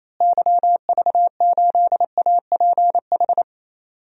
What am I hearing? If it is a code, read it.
Morse code, heard as a keyed tone, about 28 wpm, 700 Hz. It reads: YV8APH
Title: DXCC-100 Call Signs - 6 Characters - Repeated 3x - 28wpm